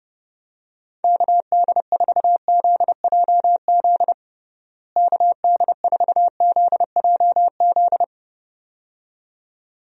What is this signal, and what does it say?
Morse code, heard as a keyed tone, about 30 wpm, 700 Hz. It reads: KD4ZJZ KD4ZJZ
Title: USA Call Signs - 6 Characters - Repeated 2x - 30wpm